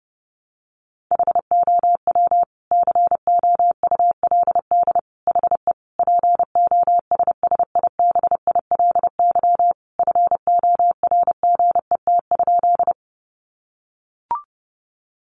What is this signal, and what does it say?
Morse code, heard as a keyed tone, about 30 wpm, 700 Hz. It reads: HOW COULD HE POSSIBILY FORGET?
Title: Sentences from Top 1300 Words 30wpm